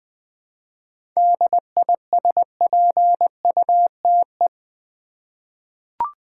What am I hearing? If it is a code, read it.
Morse code, heard as a keyed tone, about 20 wpm, 700 Hz. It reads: DISPUTE
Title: 7 Letter Words - First 3 Letters Given - 20wpm